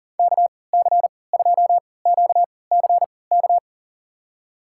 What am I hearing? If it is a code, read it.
Morse code, heard as a keyed tone, about 40 wpm, spaced as for 23 wpm, 700 Hz. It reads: KC2QCK
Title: USA Call Signs - 6 Characters - 3x Character Spacing - 40wpm